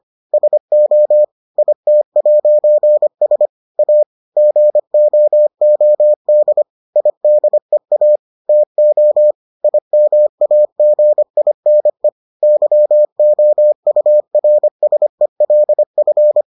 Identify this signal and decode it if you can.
Morse code, heard as a keyed tone, about 25 wpm, 600 Hz. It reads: SO IT'S A GOOD IDEA TO IMAGINE YOURSELF